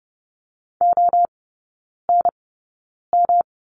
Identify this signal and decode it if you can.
Morse code, heard as a keyed tone, about 30 wpm, spaced as for 9 wpm, 700 Hz. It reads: ONM
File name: Lesson 18e - Letter m - Three Characters 30wpm